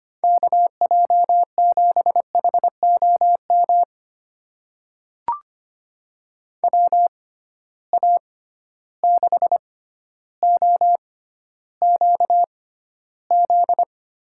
Morse code, heard as a keyed tone, about 25 wpm, spaced as for 13 wpm, 700 Hz. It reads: KJ7HOM WA6OQZ
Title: USA Call Signs - 6x Character Spacing - 25wpm